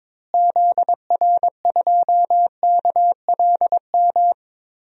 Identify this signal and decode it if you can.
Morse code, heard as a keyed tone, about 22 wpm, 700 Hz. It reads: ZR2KLM
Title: DXCC-100 Call Signs - 6 Characters - Repeated 2x - 22wpm